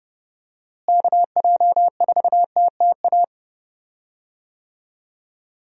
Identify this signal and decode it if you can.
Morse code, heard as a keyed tone, about 30 wpm, 700 Hz. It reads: KJ4TTA
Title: USA Call Signs - 6 Characters - Repeated 3x - 30wpm